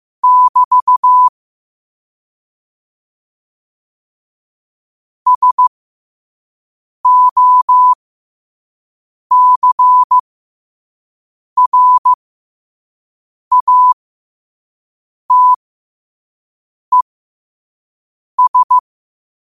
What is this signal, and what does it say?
Morse code, heard as a keyed tone, about 15 wpm, spaced as for 5 wpm, 1000 Hz. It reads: <BT> SOCRATES
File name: qotdfiles / 5wpm